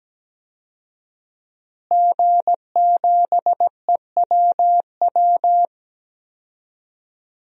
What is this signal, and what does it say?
Morse code, heard as a keyed tone, about 17 wpm, 700 Hz. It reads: G7EWW
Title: DXCC-100 Call Signs - 5 Characters - Repeated 2x - 17wpm